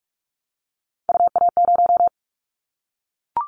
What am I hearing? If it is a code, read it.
Morse code, heard as a keyed tone, about 44 wpm, 700 Hz. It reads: UA0
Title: Callsign Prefixes for Top 100 DXCC - 45wpm